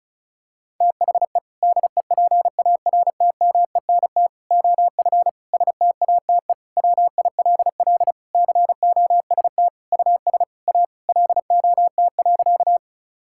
Morse code, heard as a keyed tone, about 36 wpm, 700 Hz. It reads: THE DEPARTMENT OF STATE WILL COST US A LOT.
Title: Sentences from Top 1000 Words - Review Part 5 - 35wpm